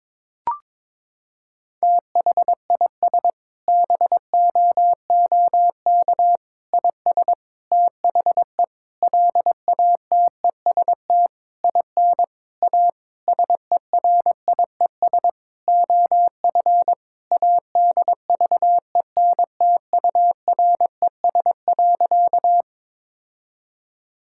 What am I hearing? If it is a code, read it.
Morse code, heard as a keyed tone, about 22 wpm, 700 Hz. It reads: THIS BOOK IS THE LATEST IN A SERIES OF ADVENTURES.
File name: Sentences with 7-Word Phrases - 22wpm